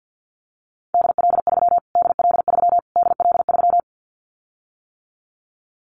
Morse code, heard as a keyed tone, about 50 wpm, 700 Hz. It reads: DL3 DL3 DL3
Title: Callsign Prefixes for Top 25 DXCC - Repeated 3x - 50wpm